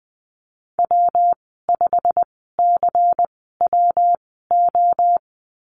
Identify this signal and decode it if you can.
Morse code, heard as a keyed tone, about 19 wpm, spaced as for 14 wpm, 700 Hz. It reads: W5CWO
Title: 154 Lesson 25g - Letter p - Call Signs 20wpm